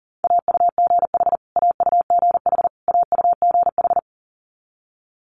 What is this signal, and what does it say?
Morse code, heard as a keyed tone, about 40 wpm, 700 Hz. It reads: AUGH AUGH AUGH
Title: Multi-letter Phonemes Repeated 3x 40wpm